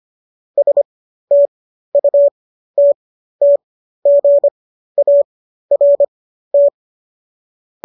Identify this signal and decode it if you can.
Morse code, heard as a keyed tone, about 25 wpm, spaced as for 13 wpm, 570 Hz. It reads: STUTTGART